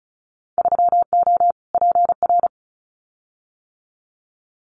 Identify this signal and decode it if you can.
Morse code, heard as a keyed tone, about 36 wpm, 700 Hz. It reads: 3O PR
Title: Winter Field Day Exchanges Encore 35wpm